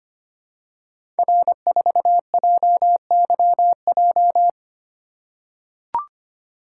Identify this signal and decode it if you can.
Morse code, heard as a keyed tone, about 25 wpm, 700 Hz. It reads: R4JYJ
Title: Code Groups - 5 Letter:Numbers - 25wpm